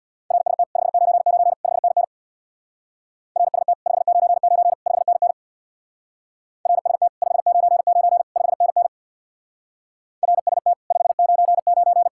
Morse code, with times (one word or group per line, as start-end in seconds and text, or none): 0.3-0.6 RST
0.8-1.5 599
1.6-2.0 5NN
3.4-3.7 RST
3.9-4.7 599
4.9-5.3 5NN
6.7-7.1 RST
7.2-8.2 599
8.4-8.9 5NN
10.2-10.7 RST
10.9-12.1 599